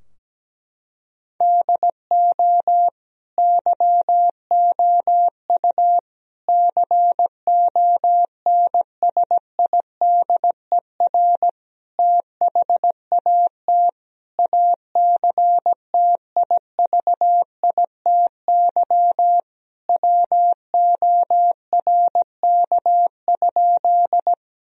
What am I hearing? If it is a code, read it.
Morse code, heard as a keyed tone, about 17 wpm, 700 Hz. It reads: DO YOU CONSIDER THAT ACTIVITY WORK?
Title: Sentences from Top 1000 Words - Review Part 4 - 17wpm